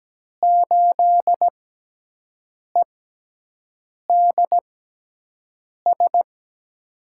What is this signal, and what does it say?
Morse code, heard as a keyed tone, about 17 wpm, spaced as for 6 wpm, 700 Hz. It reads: 8EDS
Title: USA Call Signs - 6x Character Spacing - 17wpm